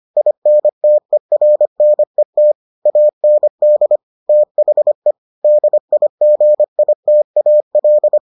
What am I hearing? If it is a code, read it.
Morse code, heard as a keyed tone, about 25 wpm, 600 Hz. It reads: INTERNET AND THE DIGITAL